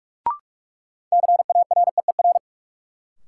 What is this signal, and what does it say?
Morse code, heard as a keyed tone, about 42 wpm, 700 Hz. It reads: CAREER
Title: Sentences from Top 1000 Words - New Word Drill - 45wpm